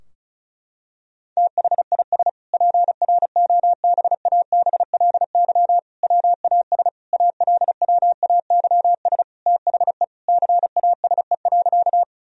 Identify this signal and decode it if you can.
Morse code, heard as a keyed tone, about 36 wpm, 700 Hz. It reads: THIS PROBABLY WAS ALWAYS THE CASE.
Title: Sentences from Top 1000 Words - Review Part 4 - 35wpm